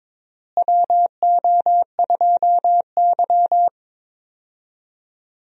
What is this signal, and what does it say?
Morse code, heard as a keyed tone, about 22 wpm, 700 Hz. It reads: WO2Y
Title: USA Call Signs - 4 Characters - Repeated 3x - 22wpm